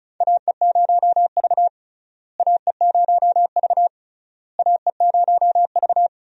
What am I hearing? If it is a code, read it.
Morse code, heard as a keyed tone, about 36 wpm, 700 Hz. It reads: AE0V AE0V AE0V
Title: USA Call Signs - 4 Characters - Repeated 3x - 35wpm